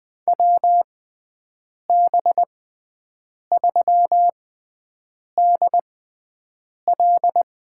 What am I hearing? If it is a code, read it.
Morse code, heard as a keyed tone, about 21 wpm, spaced as for 7 wpm, 700 Hz. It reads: WB3DL
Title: USA Call Signs - 6 Characters - 6x Character Spacing - 20wpm